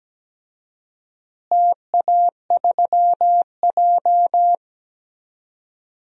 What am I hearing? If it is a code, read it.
Morse code, heard as a keyed tone, about 17 wpm, 700 Hz. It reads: TA3J
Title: DXCC-100 Call Signs - 4 Characters - Repeated 3x - 17wpm